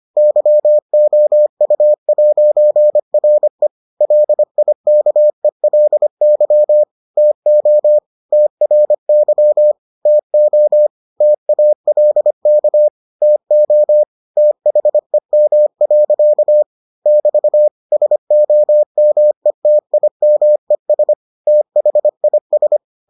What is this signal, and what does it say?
Morse code, heard as a keyed tone, about 25 wpm, 600 Hz. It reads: YOU'RE LIKELY TO TRY TO TALK TO THEM. <BT> SOMETIMES THIS